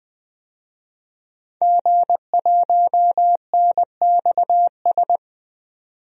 Morse code, heard as a keyed tone, about 20 wpm, 700 Hz. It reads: G1NXS
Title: DXCC-100 Call Signs - 5 Characters - 20wpm